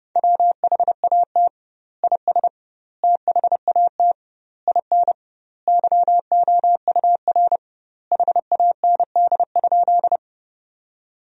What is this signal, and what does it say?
Morse code, heard as a keyed tone, about 30 wpm, 700 Hz. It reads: WHAT IS THAT IN YOUR HAND?
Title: 2x Word-Spacing - Sentences from Top 200 Words - 30wpm